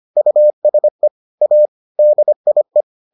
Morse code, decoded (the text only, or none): USE A DIE